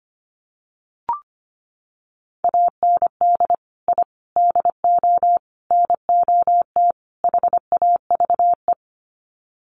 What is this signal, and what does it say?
Morse code, heard as a keyed tone, about 25 wpm, 700 Hz. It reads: AND I DO NOT HAVE
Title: Sets of 5 Words Encore 25wpm